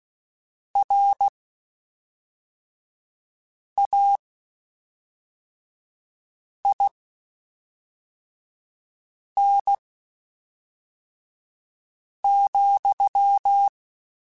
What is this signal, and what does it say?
Morse code, heard as a keyed tone, about 16 wpm, spaced as for 3 wpm, 800 Hz. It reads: RAIN,